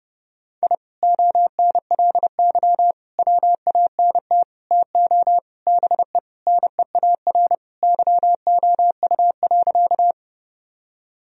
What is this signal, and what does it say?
Morse code, heard as a keyed tone, about 31 wpm, 700 Hz. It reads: I ONLY WANT TO BE NEAR YOU.
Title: Sentences from Top 1000 Words - Review Part 1 - 30wpm